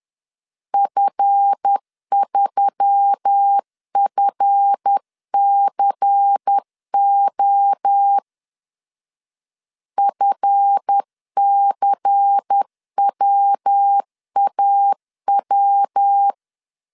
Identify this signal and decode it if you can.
Morse code, heard as a keyed tone, about 10 wpm, 800 Hz. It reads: F3FCO FCWAW
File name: LEZIONI / 7 GF9Z / 7 GF 10wpm 2514